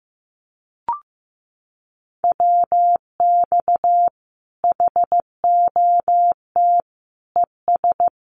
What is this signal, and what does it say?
Morse code, heard as a keyed tone, about 15 wpm, 700 Hz. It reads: WX HOT ES
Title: Common QSO Phrases 15wpm